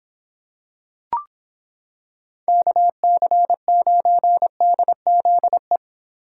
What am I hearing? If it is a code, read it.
Morse code, heard as a keyed tone, about 26 wpm, 700 Hz. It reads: KC9DZE